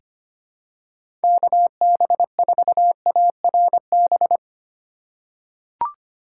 Morse code, heard as a keyed tone, about 25 wpm, 700 Hz. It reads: KB4ARB
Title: USA Call Signs - 6 Characters - 4x Character Spacing - 25wpm